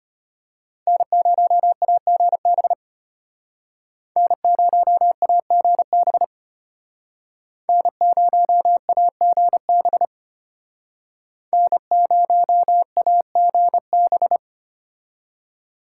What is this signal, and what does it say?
Morse code, heard as a keyed tone, about 32 wpm, 700 Hz. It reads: N0AGB N0AGB N0AGB N0AGB